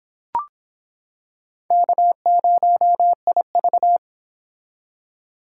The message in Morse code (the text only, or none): K0IV